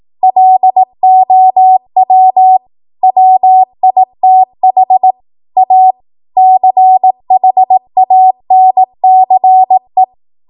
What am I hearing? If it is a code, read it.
Morse code, heard as a keyed tone, about 18 wpm, 750 Hz. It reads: LOW WITH A CHANCE
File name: Morse / 18WPM